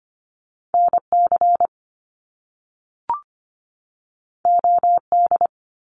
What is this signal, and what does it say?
Morse code, heard as a keyed tone, about 25 wpm, 700 Hz. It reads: NC OD
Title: Top Two Letter Combinations in Words 25wpm